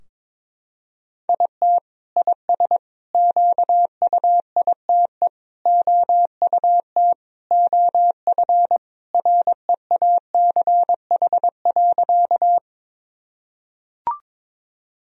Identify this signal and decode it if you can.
Morse code, heard as a keyed tone, about 22 wpm, 700 Hz. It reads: IT IS QUITE OUT OF REACH.
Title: Sentences from Top 1000 Words - Review Part 2 - 22wpm